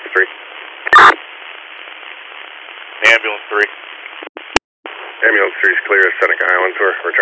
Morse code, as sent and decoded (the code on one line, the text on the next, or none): none